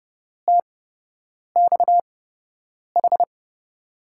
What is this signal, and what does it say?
Morse code, heard as a keyed tone, about 31 wpm, spaced as for 8 wpm, 700 Hz. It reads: TXH
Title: USA Call Signs - 6 Characters - 8x Character Spacing - 30wpm